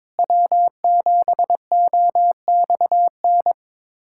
Morse code, heard as a keyed tone, about 22 wpm, 700 Hz. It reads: W7OXN
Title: USA Callsign DE Callsign - 5 Characters - 22wpm